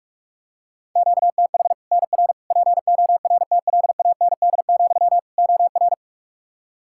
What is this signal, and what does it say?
Morse code, heard as a keyed tone, about 41 wpm, 700 Hz. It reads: QTH NR PORTLAND, OR
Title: Practice QSO - Exchange 1 - Beginning - 45wpm